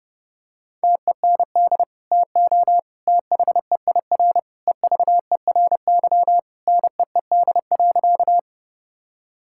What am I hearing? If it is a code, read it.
Morse code, heard as a keyed tone, about 31 wpm, 700 Hz. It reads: TEND TO THEIR EVERY NEED.
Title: Sentences from Top 1000 Words - Review Part 5 - 30wpm